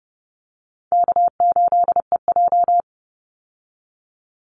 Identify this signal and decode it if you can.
Morse code, heard as a keyed tone, about 30 wpm, 700 Hz. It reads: K8EJ
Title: Call Signs Repeated 3x - 30wpm